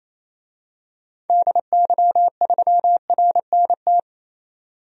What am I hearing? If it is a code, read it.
Morse code, heard as a keyed tone, about 28 wpm, 700 Hz. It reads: DY3RNT
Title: DXCC-100 Call Signs - 6 Characters - 28wpm